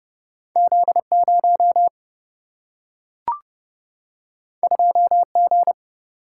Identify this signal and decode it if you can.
Morse code, heard as a keyed tone, about 30 wpm, 700 Hz. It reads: Z0 2G
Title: Single-Letter Single-Number 30wpm